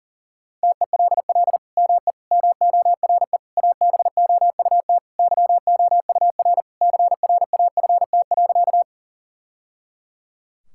Morse code, heard as a keyed tone, about 37 wpm, 700 Hz. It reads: TELL ME MORE ABOUT YOUR CRAFT.
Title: Sentences from Top 1000 Words - Review Part 6 - 40wpm